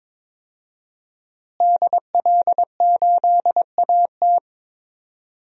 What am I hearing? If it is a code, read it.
Morse code, heard as a keyed tone, about 22 wpm, 700 Hz. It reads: DL8AT
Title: DXCC-100 Call Signs - 5 Characters - Repeated 3x - 22wpm